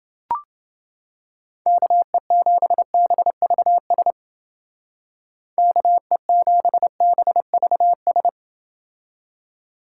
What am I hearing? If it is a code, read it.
Morse code, heard as a keyed tone, about 28 wpm, 700 Hz. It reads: KE7BVS KE7BVS